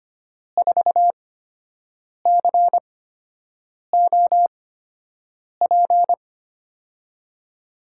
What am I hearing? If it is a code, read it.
Morse code, heard as a keyed tone, about 25 wpm, spaced as for 7 wpm, 700 Hz. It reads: 4COP
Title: USA Call Signs - 8x Character Spacing - 25wpm